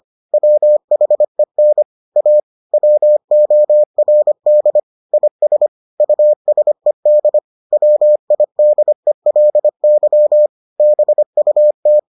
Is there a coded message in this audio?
Yes: Morse code from 0.3 to 12.0 s, heard as a keyed tone, about 25 wpm, 600 Hz. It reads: WHEN A WORD IS USED WIDELY BUT